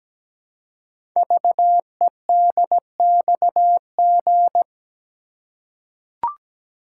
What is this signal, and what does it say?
Morse code, heard as a keyed tone, about 17 wpm, 700 Hz. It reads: VEDXG
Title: Code Groups - 5 Letters - 17wpm